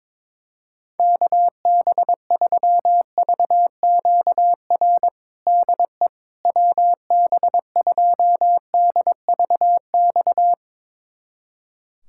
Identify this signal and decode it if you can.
Morse code, heard as a keyed tone, about 22 wpm, 700 Hz. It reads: KB3VQR DE WB2DVX
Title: USA Callsign DE Callsign - 6 Characters - 22wpm